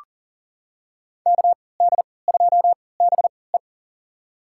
Morse code, heard as a keyed tone, about 40 wpm, spaced as for 23 wpm, 700 Hz. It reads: KD2BE
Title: USA Call Signs - 5 Characters - 3x Character Spacing - 40wpm